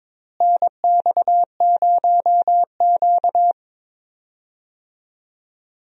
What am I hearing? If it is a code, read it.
Morse code, heard as a keyed tone, about 22 wpm, 700 Hz. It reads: NX0Q
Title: USA Call Signs - 4 Characters - Repeated 3x - 22wpm